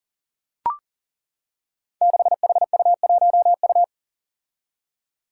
Morse code, heard as a keyed tone, about 40 wpm, 700 Hz. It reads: 6HU1U
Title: Five Challenging Letter-Number Combinations 40wpm